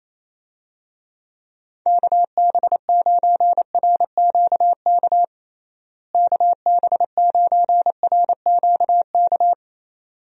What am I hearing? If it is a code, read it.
Morse code, heard as a keyed tone, about 28 wpm, 700 Hz. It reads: KB9RQK KB9RQK
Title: USA Call Signs - 6 Characters - Repeated 2x - 28wpm